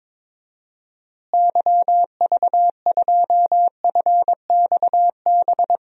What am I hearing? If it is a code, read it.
Morse code, heard as a keyed tone, about 22 wpm, 700 Hz. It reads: YV2FXB
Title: DXCC-100 Call Signs - 6 Characters - Repeated 3x - 22wpm